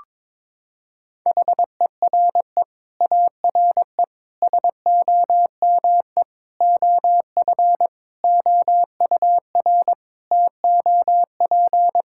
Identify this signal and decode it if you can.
Morse code, heard as a keyed tone, about 22 wpm, 700 Hz. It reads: HERE ARE SOME OF OUR TOP
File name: Sets of 9 Words - 22wpm